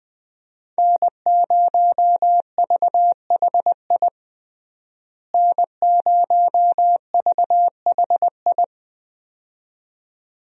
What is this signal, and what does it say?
Morse code, heard as a keyed tone, about 20 wpm, 700 Hz. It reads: N0VHI N0VHI
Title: USA Call Signs - 5 Characters - Repeated 2x - 20wpm